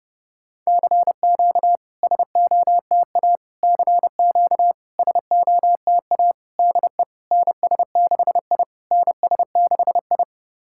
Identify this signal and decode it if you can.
Morse code, heard as a keyed tone, about 30 wpm, 690 Hz. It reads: CQ SOTA CQ SOTA DE NS6I NS6I
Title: SOTA QSO Practice 30wpm